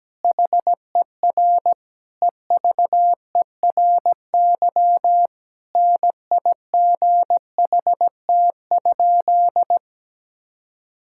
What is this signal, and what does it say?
Morse code, heard as a keyed tone, about 17 wpm, 700 Hz. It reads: HER EVERY NIGHT?
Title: Sentences from Top 1000 Words - Review Part 5 - 17wpm